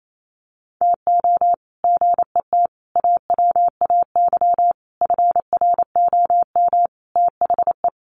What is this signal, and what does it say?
Morse code, heard as a keyed tone, about 28 wpm, 700 Hz. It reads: TO GET AWAY FROM THE
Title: Sets of 5 Words Encore2 28wpm